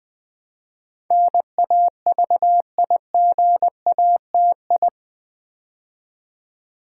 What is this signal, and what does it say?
Morse code, heard as a keyed tone, about 20 wpm, 700 Hz. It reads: NAVIGATI
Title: Build A Word - 4 Letters At A Time - 10-Letter Words - 20wpm